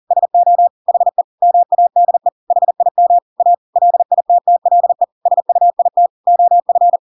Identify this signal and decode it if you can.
Morse code, heard as a keyed tone, about 40 wpm, 700 Hz. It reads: SO HE MADE HIM A LITTLE SUIT OF
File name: Pinocchio / Pinocchio ch8 40 40 ik1hge